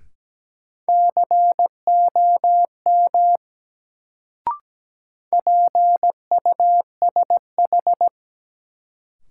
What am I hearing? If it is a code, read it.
Morse code, heard as a keyed tone, about 17 wpm, 700 Hz. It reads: COM PUSH
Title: Sentences from Top 1300 Words - New Word Drill Alt - 17wpm